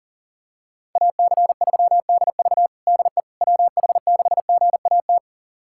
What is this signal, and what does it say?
Morse code, heard as a keyed tone, about 40 wpm, 700 Hz. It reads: AC3DV DE WH6GAT
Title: USA Callsign DE Callsign 40wpm